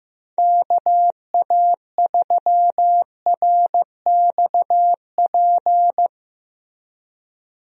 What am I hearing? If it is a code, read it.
Morse code, heard as a keyed tone, about 16 wpm, 700 Hz. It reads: KA3RXP
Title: USA Call Signs - 6 Characters - Repeated 3x - 15wpm